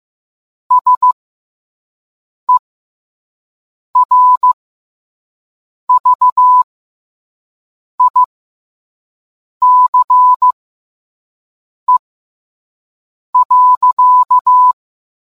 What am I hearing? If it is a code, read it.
Morse code, heard as a keyed tone, about 15 wpm, spaced as for 5 wpm, 1000 Hz. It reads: SERVICE.